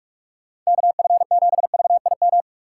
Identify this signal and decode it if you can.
Morse code, heard as a keyed tone, about 46 wpm, 700 Hz. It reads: KF7VIM
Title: USA Call Signs - 6 Characters - 6x Character Spacing - 45wpm